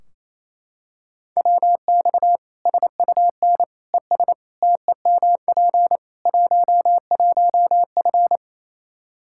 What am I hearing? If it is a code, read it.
Morse code, heard as a keyed tone, about 28 wpm, 700 Hz. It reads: WX SUN ES TEMP 11F
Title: Practice QSO - Exchange 2 - Info - 28wpm